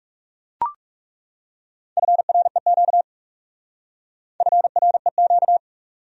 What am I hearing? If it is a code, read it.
Morse code, heard as a keyed tone, about 38 wpm, 700 Hz. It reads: FREQ FREQ